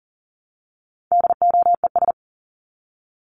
Morse code, heard as a keyed tone, about 40 wpm, 700 Hz. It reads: DOES
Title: ES:S Suffix - Top 100 Words with Base Words 40wpm